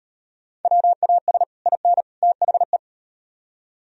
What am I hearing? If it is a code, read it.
Morse code, heard as a keyed tone, about 38 wpm, 700 Hz. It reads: WAS IN THE